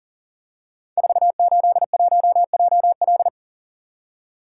Morse code, heard as a keyed tone, about 40 wpm, 690 Hz. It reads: 481JL
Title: Code Groups - 5 Letter:Numbers - 40wpm